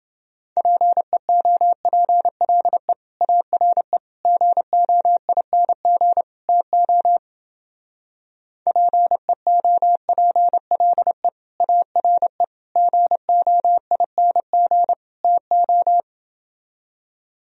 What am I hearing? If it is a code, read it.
Morse code, heard as a keyed tone, about 31 wpm, 700 Hz. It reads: PEOPLE ARE GOING TO PEOPLE ARE GOING TO